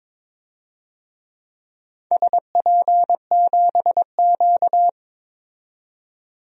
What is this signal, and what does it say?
Morse code, heard as a keyed tone, about 22 wpm, 700 Hz. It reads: SP7Q
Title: DXCC-100 Call Signs - 4 Characters - 22wpm